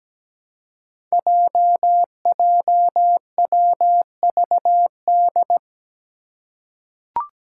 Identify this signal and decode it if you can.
Morse code, heard as a keyed tone, about 17 wpm, 700 Hz. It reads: JJWVD
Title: Code Groups - 5 Letters - 17wpm